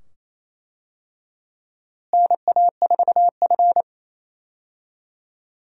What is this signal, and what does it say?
Morse code, heard as a keyed tone, about 28 wpm, 700 Hz. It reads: NA4F
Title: USA Call Signs - 4 Characters - Repeated 2x - 28wpm